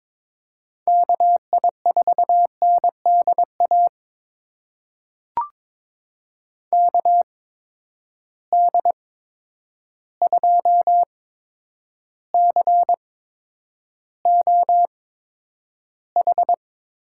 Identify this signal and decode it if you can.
Morse code, heard as a keyed tone, about 21 wpm, spaced as for 9 wpm, 700 Hz. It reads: KI4NDA KD2COH
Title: USA Call Signs - 6 Characters - 8x Character Spacing - 22wpm